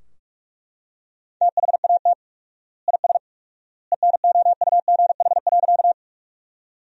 Morse code, heard as a keyed tone, about 42 wpm, 700 Hz. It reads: THAT IS ENOUGH.
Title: 4x Word-Spacing - Sentences from Top 400 Words - 45wpm